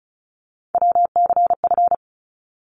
Morse code, heard as a keyed tone, about 36 wpm, 700 Hz. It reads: WCF
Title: ARRL Sections 35wpm